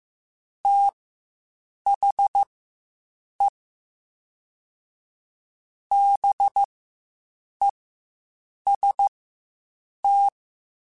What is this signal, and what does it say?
Morse code, heard as a keyed tone, about 15 wpm, spaced as for 7 wpm, 800 Hz. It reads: THE BEST
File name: Quotes Spaced / Quotes Spaced-mp3s / Temper sp @10